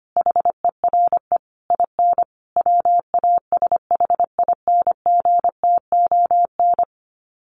Morse code, heard as a keyed tone, about 25 wpm, 700 Hz. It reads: HERE IN WASHINGTON
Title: Sets of 5 Words Encore2 25wpm